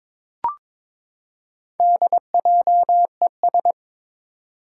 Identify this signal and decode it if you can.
Morse code, heard as a keyed tone, about 21 wpm, 700 Hz. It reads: DJES
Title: Four Challenging Letter-Number Combinations 22wpm